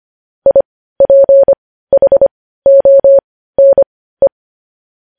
Morse code, heard as a keyed tone, about 25 wpm, spaced as for 15 wpm, 550 Hz. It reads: IPHONE
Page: news headlines in morse code at 15 wpm